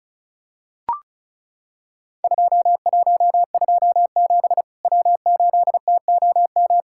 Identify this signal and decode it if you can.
Morse code, heard as a keyed tone, about 34 wpm, 700 Hz. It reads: 2127 W8TOM
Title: Worked All Europe DX Contest - QTC Exchange - 35wpm